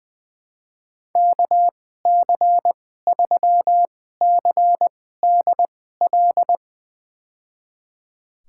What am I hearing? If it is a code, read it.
Morse code, heard as a keyed tone, about 21 wpm, spaced as for 15 wpm, 700 Hz. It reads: KC3CDL
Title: USA Call Signs - 6 Characters - 2x Character Spacing - 20wpm